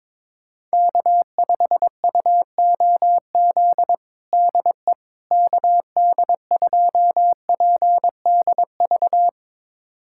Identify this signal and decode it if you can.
Morse code, heard as a keyed tone, about 22 wpm, 700 Hz. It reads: K5UOZ DE KD2PDV
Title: USA Callsign DE Callsign 22wpm